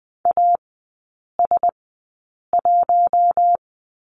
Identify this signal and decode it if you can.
Morse code, heard as a keyed tone, about 20 wpm, spaced as for 8 wpm, 700 Hz. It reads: AS1
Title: 041 Lesson 8e - Number 1 - Three Characters 20wpm